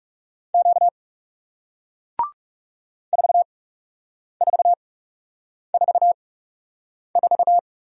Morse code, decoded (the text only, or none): Q 4444